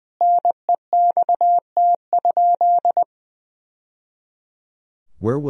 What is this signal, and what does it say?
Morse code, heard as a keyed tone, about 20 wpm, 700 Hz. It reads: NEXT?
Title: Sentences from Top 1000 Words - Review Part 2 - 20wpm